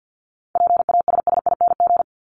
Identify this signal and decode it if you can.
Morse code, heard as a keyed tone, about 50 wpm, 700 Hz. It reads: PASSING
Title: ING Suffix - Top 100 Words with Base Words 50wpm